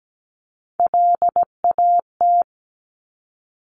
Morse code, heard as a keyed tone, about 17 wpm, 700 Hz. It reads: LAT
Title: Top Three Letter Combinations in Words 17wpm